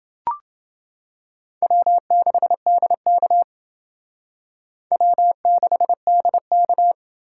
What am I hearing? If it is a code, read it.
Morse code, heard as a keyed tone, about 28 wpm, 700 Hz. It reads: W6DK W6DK